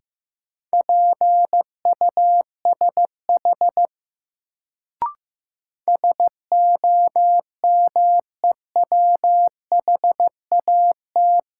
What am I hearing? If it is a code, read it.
Morse code, heard as a keyed tone, about 15 wpm, 700 Hz. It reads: PUSH SOMEWHAT
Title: Sentences from Top 1300 Words - New Word Drill Alt - 15wpm